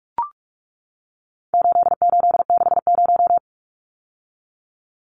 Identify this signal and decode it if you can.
Morse code, heard as a keyed tone, about 44 wpm, 700 Hz. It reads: 8860
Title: Numbers - 4-Digit 45wpm